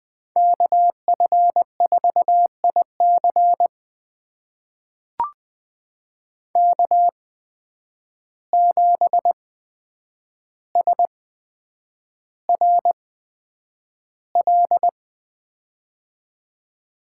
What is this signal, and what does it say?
Morse code, heard as a keyed tone, about 21 wpm, spaced as for 9 wpm, 700 Hz. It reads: KF4IC K7SRL
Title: USA Call Signs - 5 Characters - 8x Character Spacing - 20wpm